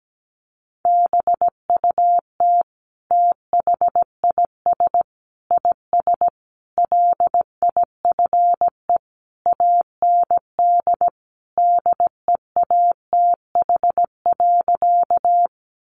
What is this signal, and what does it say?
Morse code, heard as a keyed tone, about 17 wpm, 700 Hz. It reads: BUT THIS IS LIFE AND DEATH.
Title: Sentences from Top 1100 Words 17wpm